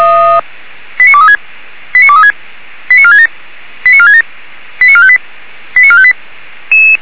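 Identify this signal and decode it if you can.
Five-tone selective call (ZVEI), heard as quick runs of five tones, tones of about 70 ms. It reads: 89236 89236 89467 89467 89458 89458
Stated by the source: ZVEI five-tone call received on SDR.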